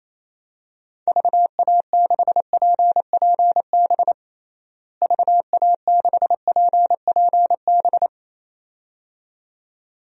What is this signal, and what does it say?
Morse code, heard as a keyed tone, about 28 wpm, 700 Hz. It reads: VA6PPB VA6PPB
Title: DXCC-100 Call Signs - 6 Characters - Repeated 2x - 28wpm